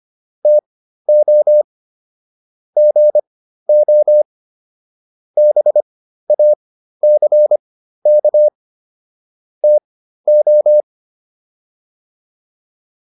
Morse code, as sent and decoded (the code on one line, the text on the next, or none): - --- / --. --- / -... .- -.-. -.- / - ---
TO GO BACK TO